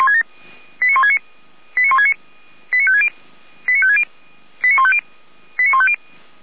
Five-tone selective call (ZVEI), heard as quick runs of five tones, tones of about 70 ms. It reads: 89169 89169 89560 89560 89150 89150